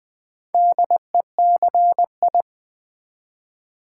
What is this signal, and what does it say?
Morse code, heard as a keyed tone, about 20 wpm, 700 Hz. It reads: DECI
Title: Build A Word - 2 Letters At A Time - 8-Letter Words - 20wpm